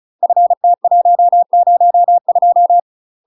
Morse code, heard as a keyed tone, about 36 wpm, 700 Hz. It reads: FT102
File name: UK MASTER QSOs TRAINING / 35WPM / UK MASTER QSOs 0009